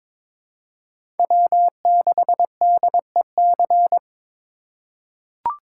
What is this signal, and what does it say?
Morse code, heard as a keyed tone, about 22 wpm, 700 Hz. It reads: W6DEC
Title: USA Call Signs - 5 Characters - 2x Character Spacing - 22wpm